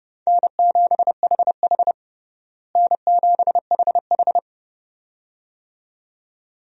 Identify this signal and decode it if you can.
Morse code, heard as a keyed tone, about 30 wpm, 700 Hz. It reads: N7HH N7HH
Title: USA Call Signs - 4 Characters - Repeated 2x - 30wpm